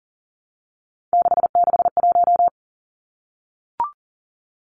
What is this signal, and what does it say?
Morse code, heard as a keyed tone, about 40 wpm, 700 Hz. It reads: 661
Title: Numbers - 3-Digit 40wpm